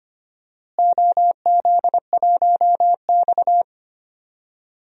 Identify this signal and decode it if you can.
Morse code, heard as a keyed tone, about 25 wpm, 700 Hz. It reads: OZ1X
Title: DXCC-100 Call Signs - 4 Characters - Repeated 3x - 25wpm